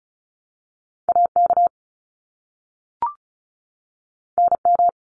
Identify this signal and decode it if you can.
Morse code, heard as a keyed tone, about 34 wpm, 700 Hz. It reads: AK NM
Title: ARRL Sections 35wpm